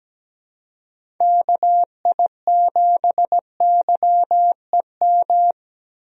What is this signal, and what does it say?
Morse code, heard as a keyed tone, about 17 wpm, 700 Hz. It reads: KI7YEM
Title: USA Call Signs - 6 Characters - 17wpm